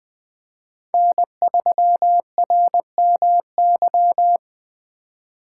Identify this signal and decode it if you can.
Morse code, heard as a keyed tone, about 20 wpm, 700 Hz. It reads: N3RMY
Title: Speed Racing - USA 1x3 Call Signs 17wpm